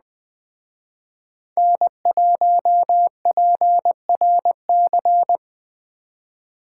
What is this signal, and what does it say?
Morse code, heard as a keyed tone, about 20 wpm, 700 Hz. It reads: N1PRC